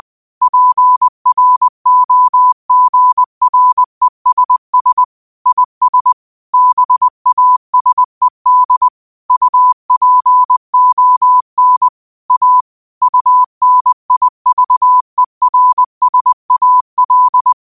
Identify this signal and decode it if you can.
Morse code, heard as a keyed tone, about 20 wpm, 1000 Hz. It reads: PROGRESS IS BASED UPON A UNIVERSAL
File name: qotdfiles / 20wpm